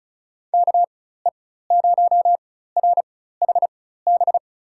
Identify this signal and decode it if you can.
Morse code, heard as a keyed tone, about 34 wpm, spaced as for 16 wpm, 700 Hz. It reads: KE0RHB
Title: USA Call Signs - 6 Characters - 4x Character Spacing - 35wpm